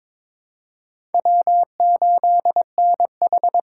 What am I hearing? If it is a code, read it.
Morse code, heard as a keyed tone, about 22 wpm, 700 Hz. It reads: W8NH
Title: USA Call Signs - 4 Characters - 2x Character Spacing - 22wpm